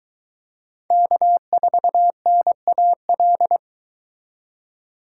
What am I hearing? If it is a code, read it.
Morse code, heard as a keyed tone, about 23 wpm, 700 Hz. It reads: K4NAL